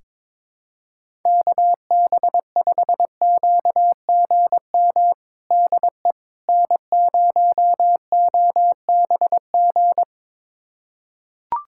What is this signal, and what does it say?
Morse code, heard as a keyed tone, about 22 wpm, 700 Hz. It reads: KB5QGM DE N0OBG
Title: USA Callsign DE Callsign 22wpm